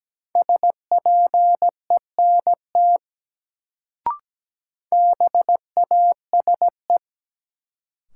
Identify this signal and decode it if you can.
Morse code, heard as a keyed tone, about 17 wpm, 700 Hz. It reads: SPENT BASE
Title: Sets of 5 Words - New Word Drill - 17wpm